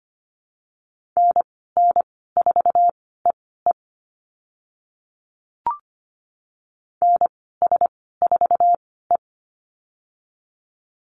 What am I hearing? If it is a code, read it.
Morse code, heard as a keyed tone, about 25 wpm, spaced as for 16 wpm, 700 Hz. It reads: NN4EE NS4E
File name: Lesson 15g - Number 5 - Call Signs 25wpm